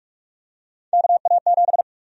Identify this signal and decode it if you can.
Morse code, heard as a keyed tone, about 46 wpm, 700 Hz. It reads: KA7
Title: USA Call Sign Prefixes - Repeated 2x - 45wpm